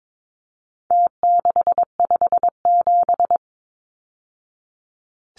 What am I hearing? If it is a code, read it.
Morse code, heard as a keyed tone, about 22 wpm, 700 Hz. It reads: T657
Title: Numbers - Contest Style - 4-Digit 22wpm